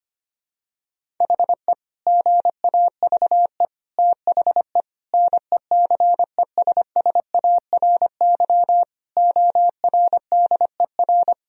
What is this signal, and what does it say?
Morse code, heard as a keyed tone, about 25 wpm, 700 Hz. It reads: HE GAVE THE NECESSARY ORDER
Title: Sentences from Top 1000 Words - Review Part 2 - 25wpm